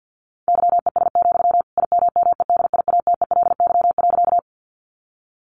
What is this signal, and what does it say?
Morse code, heard as a keyed tone, about 50 wpm, 700 Hz. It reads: YES, IMMEDIATELY.
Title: Sentences from Top 1100 Words 50wpm